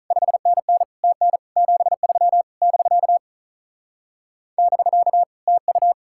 Morse code, heard as a keyed tone, about 43 wpm, 700 Hz. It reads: 5NN TN 73 <BK> <BK> TU
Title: Speed-Racing Common QSO Phrases - 35wpm